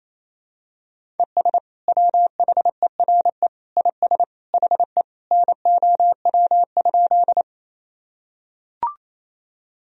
Morse code, heard as a keyed tone, about 28 wpm, 700 Hz. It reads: ES WHERE IS HE NOW?
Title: Sentences from Top 200 Words and Abbreviations 28wpm